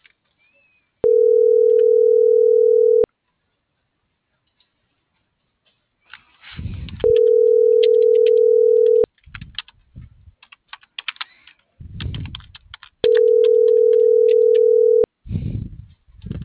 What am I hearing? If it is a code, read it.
Ringback tone (North America), 440 + 480 Hz.